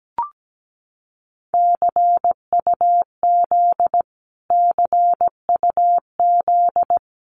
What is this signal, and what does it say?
Morse code, heard as a keyed tone, about 17 wpm, 700 Hz. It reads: CUZ CUZ
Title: Abbreviations Repeated 3x - 17wpm